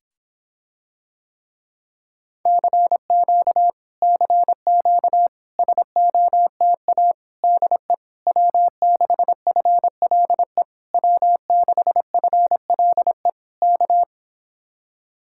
Morse code, heard as a keyed tone, about 26 wpm, 700 Hz. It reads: CQ CQ SOTA DE W6FLE W6FLE K